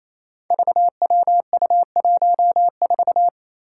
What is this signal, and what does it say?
Morse code, heard as a keyed tone, about 28 wpm, 700 Hz. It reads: VWU14
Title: Five Challenging Letter-Number Combinations 28wpm